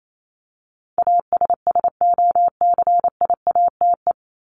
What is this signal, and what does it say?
Morse code, heard as a keyed tone, about 28 wpm, 700 Hz. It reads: ASSOCIATE
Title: Top 1751-2000 Words 28wpm